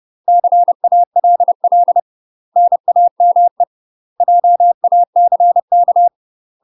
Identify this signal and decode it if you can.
Morse code, heard as a keyed tone, about 30 wpm, 700 Hz. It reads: CALL NAME JACK
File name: UK MASTER QSOs TRAINING / 30WPM / UK MASTER QSOs 0022